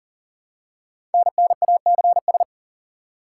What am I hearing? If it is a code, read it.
Morse code, heard as a keyed tone, about 44 wpm, 700 Hz. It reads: NNACS
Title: Code Groups - 5 Letters - 40wpm